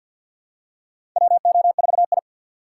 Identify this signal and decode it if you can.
Morse code, heard as a keyed tone, about 52 wpm, 700 Hz. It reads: WO4I
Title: USA Call Signs - 4 Characters - 3x Character Spacing - 50wpm